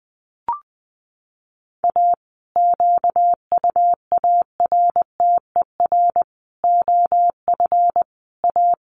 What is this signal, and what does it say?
Morse code, heard as a keyed tone, about 20 wpm, 700 Hz. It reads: A QUARTER OF A
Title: Sets of 5 Words Encore2 20wpm